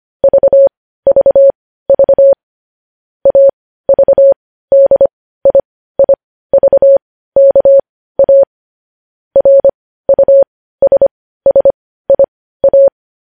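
Morse code, heard as a keyed tone, about 25 wpm, spaced as for 15 wpm, 550 Hz. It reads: VVV AVDIIVKA RUSSIA
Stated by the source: Morse news headlines sent at 15 WPM, beginning with vvv.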